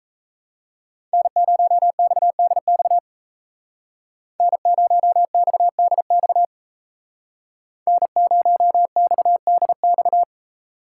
Morse code, heard as a keyed tone, about 36 wpm, 700 Hz. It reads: N0XDX N0XDX N0XDX